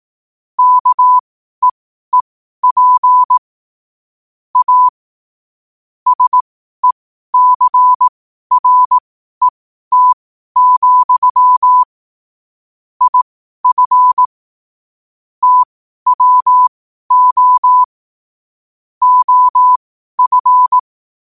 Morse code, heard as a keyed tone, about 18 wpm, spaced as for 12 wpm, 1000 Hz. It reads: KEEP A SECRET, IF TWO OF
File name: qotdfiles / 12wpm